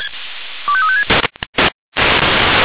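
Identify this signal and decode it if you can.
Five-tone selective call (ZVEI), heard as quick runs of five tones, tones of about 70 ms. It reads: 25356
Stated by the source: ZVEI five-tone selective call received with an SDR.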